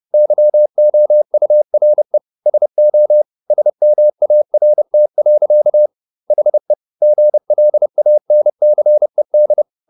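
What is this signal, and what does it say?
Morse code, heard as a keyed tone, about 30 wpm, 600 Hz. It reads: YOURE SO SMART. HE GLANCED
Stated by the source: Morse code at 30 wpm, the novel Never by Ken Follett in English, from CW med Gnister.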